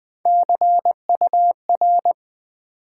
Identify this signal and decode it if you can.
Morse code, heard as a keyed tone, about 20 wpm, 700 Hz. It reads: CUR
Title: Build A Word - 1 Letter At A Time - 5-Letter Words - 20wpm